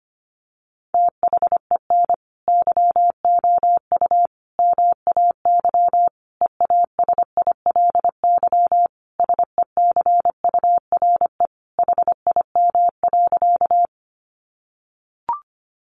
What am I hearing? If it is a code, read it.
Morse code, heard as a keyed tone, about 25 wpm, 700 Hz. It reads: THEN YOU MAY EASILY SECURE HIM.
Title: Sentences from Top 1200 Words 25wpm